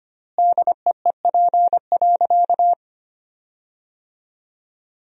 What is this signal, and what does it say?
Morse code, heard as a keyed tone, about 25 wpm, 700 Hz. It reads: DEEP.
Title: 4x Word-Spacing - Sentences from Top 500 Words - 25wpm